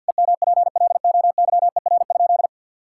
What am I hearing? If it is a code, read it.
Morse code, heard as a keyed tone, about 52 wpm, 700 Hz. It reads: EMPLOYER?
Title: Sentences from Top 1600 Words 50wpm